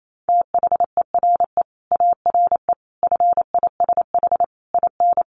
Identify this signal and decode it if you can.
Morse code, heard as a keyed tone, about 28 wpm, 700 Hz. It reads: THERE ARE FISH IN
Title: Sentences from Top 1100 Words 28wpm